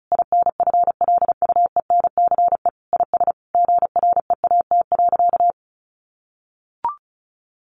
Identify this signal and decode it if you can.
Morse code, heard as a keyed tone, about 34 wpm, 700 Hz. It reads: INFLUENCE IS GREAT.
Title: Sentences from Top 1100 Words 35wpm